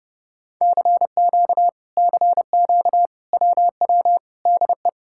Morse code, heard as a keyed tone, about 30 wpm, 700 Hz. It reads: CQ CQ WW DE